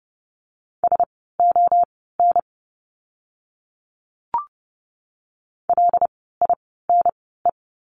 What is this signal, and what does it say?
Morse code, heard as a keyed tone, about 30 wpm, spaced as for 17 wpm, 700 Hz. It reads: SON LINE
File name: Lesson 16f - Letter c - Words 30wpm